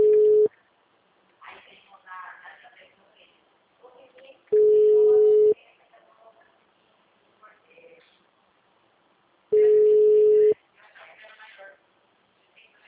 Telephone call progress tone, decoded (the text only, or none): ringback tone